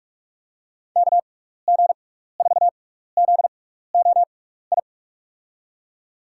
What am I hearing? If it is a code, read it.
Morse code, heard as a keyed tone, about 45 wpm, spaced as for 15 wpm, 700 Hz. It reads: KG4ZOI
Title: USA Call Signs - 6 Characters - 6x Character Spacing - 45wpm